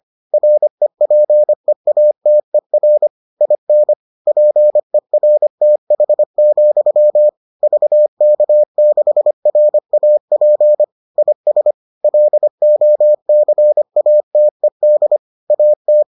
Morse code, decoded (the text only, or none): REPEATER IN PERTH, VK6RAP IS LOCATED AT